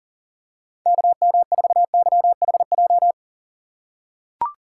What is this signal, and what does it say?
Morse code, heard as a keyed tone, about 38 wpm, 700 Hz. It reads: KM4YHJ
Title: USA Call Signs - 6 Characters - 4x Character Spacing - 40wpm